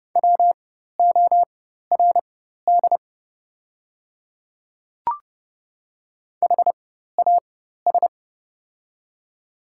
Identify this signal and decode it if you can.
Morse code, heard as a keyed tone, about 30 wpm, spaced as for 14 wpm, 690 Hz. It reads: WORD HAS
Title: 4x Character Spacing - Top 100 Words - 30wpm